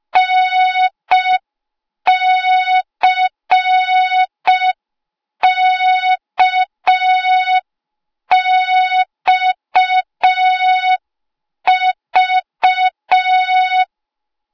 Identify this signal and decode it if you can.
Morse code, heard as a keyed tone, about 5 wpm, 3700 Hz. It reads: NCKXV